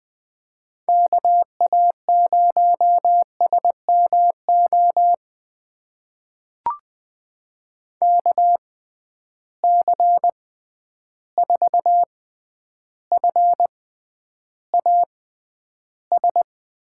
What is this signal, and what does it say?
Morse code, heard as a keyed tone, about 20 wpm, spaced as for 10 wpm, 700 Hz. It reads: KA0SMO KC4FAS
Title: USA Call Signs - 6 Characters - 6x Character Spacing - 20wpm